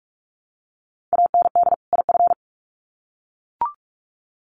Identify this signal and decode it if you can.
Morse code, heard as a keyed tone, about 44 wpm, 700 Hz. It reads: AND IF
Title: Sets of 2 Words 45wpm